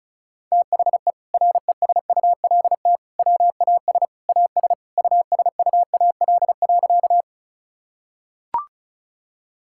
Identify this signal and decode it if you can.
Morse code, heard as a keyed tone, about 34 wpm, 690 Hz. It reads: THE RESULT WAS AS USUAL.
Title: Sentences from Top 1000 Words - Review Part 5 - 35wpm